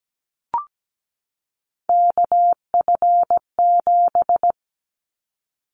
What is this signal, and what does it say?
Morse code, heard as a keyed tone, about 17 wpm, 700 Hz. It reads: KF7
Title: Callsign Prefixes for Top 25 DXCC - 17wpm